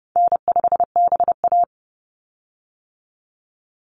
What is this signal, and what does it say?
Morse code, heard as a keyed tone, about 30 wpm, 700 Hz. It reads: N5BA
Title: Easy Call Signs - 30wpm